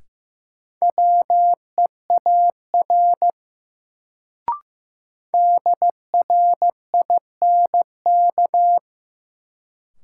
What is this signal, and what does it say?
Morse code, heard as a keyed tone, about 15 wpm, 700 Hz. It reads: WEAR DRINK
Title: Sentences from Top 1500 Words - New Word Drill Alt - 15wpm